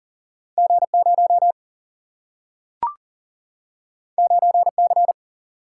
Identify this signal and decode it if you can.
Morse code, heard as a keyed tone, about 40 wpm, 700 Hz. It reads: G 0 9 C
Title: Single-Letter Single-Number 40wpm